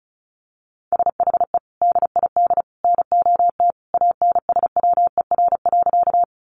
Morse code, heard as a keyed tone, about 34 wpm, 700 Hz. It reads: SHE DID NOT ANSWER.
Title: Sentences from Top 600 Words 35wpm